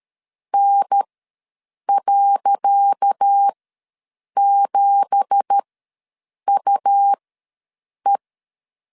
Morse code, heard as a keyed tone, about 13 wpm, spaced as for 7 wpm, 800 Hz. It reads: N.7UE
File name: LEZIONI / 10 SIGNS / 10C DOT ws 12wpm 3548